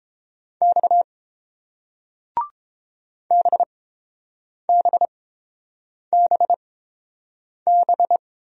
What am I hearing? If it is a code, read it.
Morse code, heard as a keyed tone, about 31 wpm, spaced as for 7 wpm, 700 Hz. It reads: X BBBB